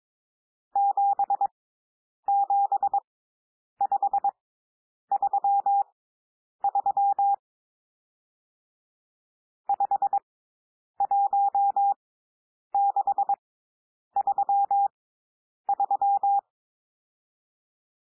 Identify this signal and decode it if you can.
Morse code, heard as a keyed tone, about 22 wpm, spaced as for 9 wpm, 800 Hz. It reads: 77533 51633
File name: cwmp3 / N1 SOUND NUMBERS 22 wpm